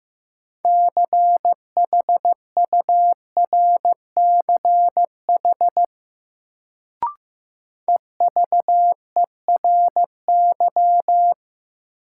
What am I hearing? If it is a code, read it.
Morse code, heard as a keyed tone, about 15 wpm, 700 Hz. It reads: CHURCH EVERY
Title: Sets of 5 Words - New Word Drill - 15wpm